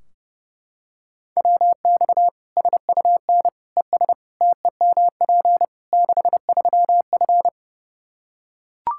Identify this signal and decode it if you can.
Morse code, heard as a keyed tone, about 30 wpm, 700 Hz. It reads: WX SUN ES TEMP 63F
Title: Practice QSO - Exchange 2 - Info - 30wpm